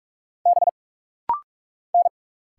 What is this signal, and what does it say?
Morse code, heard as a keyed tone, about 46 wpm, 700 Hz. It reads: B N
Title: Single Letters - Rapid-Fire 45wpm